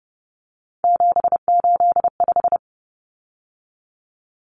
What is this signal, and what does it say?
Morse code, heard as a keyed tone, about 30 wpm, 700 Hz. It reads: 785
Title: Numbers - 3-Digit 30wpm